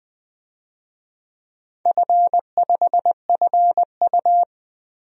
Morse code, heard as a keyed tone, about 21 wpm, 700 Hz. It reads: F5FU
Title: DXCC-100 Call Signs - 4 Characters - Repeated 2x - 20wpm